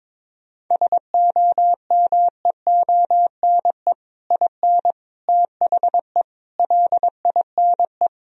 Morse code, heard as a keyed tone, about 21 wpm, 700 Hz. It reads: SOMEONE IN THE LINE
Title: Sentences from Top 1000 Words - Review Part 1 - 22wpm